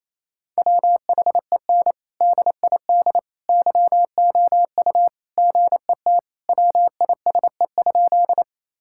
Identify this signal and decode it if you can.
Morse code, heard as a keyed tone, about 28 wpm, 700 Hz. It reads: WHEN DID YOU GET WISE?
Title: Sentences from Top 1600 Words 28wpm